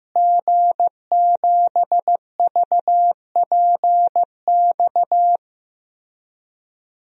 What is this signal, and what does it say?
Morse code, heard as a keyed tone, about 15 wpm, 700 Hz. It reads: G7VPX
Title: DXCC-100 Call Signs - 5 Characters - Repeated 3x - 15wpm